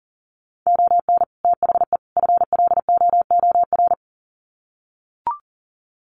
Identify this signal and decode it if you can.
Morse code, heard as a keyed tone, about 40 wpm, 700 Hz. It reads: ON THE FLOOR
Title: Sets of 3 Words 40wpm